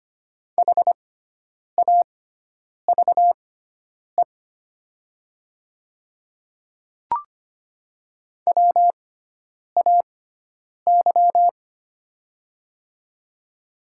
Morse code, heard as a keyed tone, about 25 wpm, spaced as for 9 wpm, 700 Hz. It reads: HAVE WAY
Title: 6x Character Spacing - Top 100 Words - 25wpm